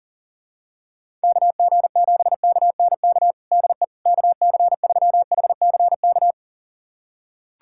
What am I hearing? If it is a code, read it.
Morse code, heard as a keyed tone, about 40 wpm, 700 Hz. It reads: KG7KNK DE KC3HCK
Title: USA Callsign DE Callsign - 6 Characters - 40wpm